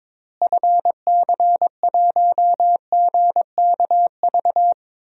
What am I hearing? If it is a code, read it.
Morse code, heard as a keyed tone, about 22 wpm, 700 Hz. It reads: FC1GKV
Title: DXCC-100 Call Signs - 6 Characters - Repeated 2x - 22wpm